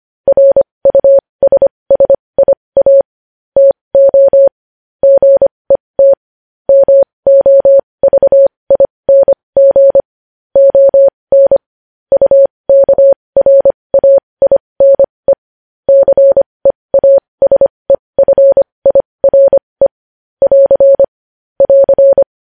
Morse code, heard as a keyed tone, about 25 wpm, spaced as for 20 wpm, 550 Hz. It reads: RUSSIA TO GET MOVING ON UKRAINE CEASEFIRE <AR> <AR>
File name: podcast / News headlines-1x-25x20